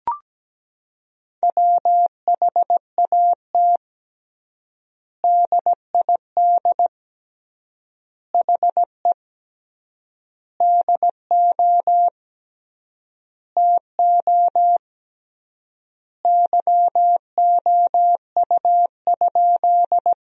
Morse code, heard as a keyed tone, about 17 wpm, 700 Hz. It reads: WHAT DID HE DO TO YOU?
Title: 3x Word-Spacing - Sentences from Top 100 Words - 17wpm